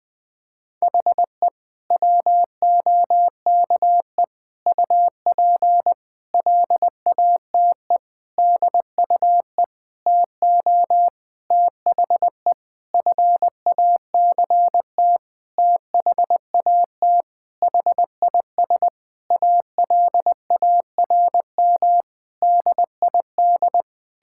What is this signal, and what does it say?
Morse code, heard as a keyed tone, about 21 wpm, 700 Hz. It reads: HE WOKE UP LATE DUE TO THE FACT THAT HIS ALARM DID
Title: Sentences with 4-Word Phrases - 20wpm